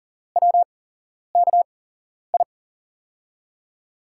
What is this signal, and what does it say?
Morse code, heard as a keyed tone, about 42 wpm, spaced as for 11 wpm, 700 Hz. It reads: WKI
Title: USA Call Signs - 6 Characters - 8x Character Spacing - 40wpm